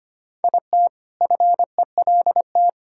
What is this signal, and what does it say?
Morse code, heard as a keyed tone, about 25 wpm, 700 Hz. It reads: IT FELT
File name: practice / 25wpm / All-Age Appropriate Jokes - 25wpm